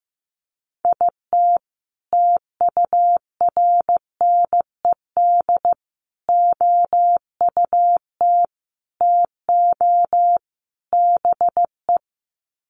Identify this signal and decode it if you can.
Morse code, heard as a keyed tone, about 15 wpm, 700 Hz. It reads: IT TURNED OUT TO BE
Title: Sets of 5 Words 15wpm